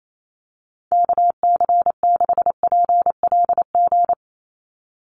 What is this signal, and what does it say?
Morse code, heard as a keyed tone, about 28 wpm, 700 Hz. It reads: KC6PLG
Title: USA Call Signs 28wpm